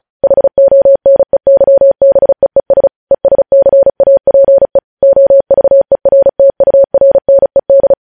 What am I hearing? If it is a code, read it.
Morse code, heard as a keyed tone, about 36 wpm, 550 Hz. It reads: HONEYBEES ESCAPE OVERTURNED
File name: podcast / News headlines-1x-35x35